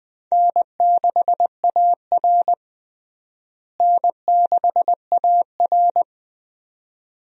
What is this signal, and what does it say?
Morse code, heard as a keyed tone, about 20 wpm, 700 Hz. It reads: N6AR N6AR
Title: USA Call Signs - 4 Characters - Repeated 2x - 20wpm